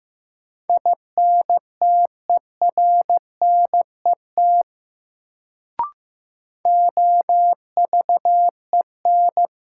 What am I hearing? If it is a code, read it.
Morse code, heard as a keyed tone, about 15 wpm, 700 Hz. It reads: INTERNET OVEN
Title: Sentences from Top 1000 Words - Review Part 4 - New Word Drill - 15wpm